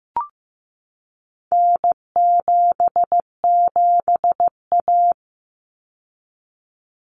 Morse code, heard as a keyed tone, about 15 wpm, 700 Hz. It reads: N77A
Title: Common Cut Numbers - 4-Digit 15wpm